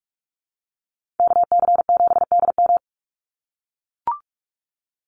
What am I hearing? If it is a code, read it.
Morse code, heard as a keyed tone, about 44 wpm, 700 Hz. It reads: KC7DM
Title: NA SOTA Activator Callsigns 45wpm